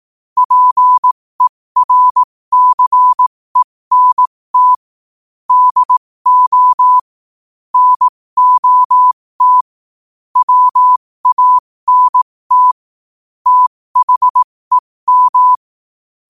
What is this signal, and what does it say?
Morse code, heard as a keyed tone, about 18 wpm, spaced as for 15 wpm, 1000 Hz. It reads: PERCENT DO NOT WANT THEM
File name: qotdfiles / 15wpm